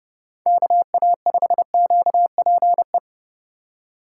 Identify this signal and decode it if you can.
Morse code, heard as a keyed tone, about 29 wpm, 700 Hz. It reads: KA5QPE